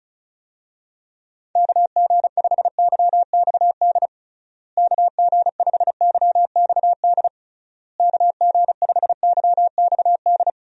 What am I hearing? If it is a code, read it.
Morse code, heard as a keyed tone, about 34 wpm, 700 Hz. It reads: KG5YXD KG5YXD KG5YXD
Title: USA Call Signs - 6 Characters - Repeated 3x - 35wpm